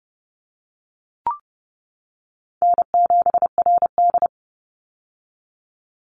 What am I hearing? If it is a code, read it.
Morse code, heard as a keyed tone, about 32 wpm, 700 Hz. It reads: N7RD
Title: Call Signs - 30wpm